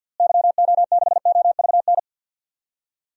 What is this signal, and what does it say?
Morse code, heard as a keyed tone, about 53 wpm, 700 Hz. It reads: YO6OVN
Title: DXCC-100 Call Signs - 6 Characters - Repeated 2x - 50wpm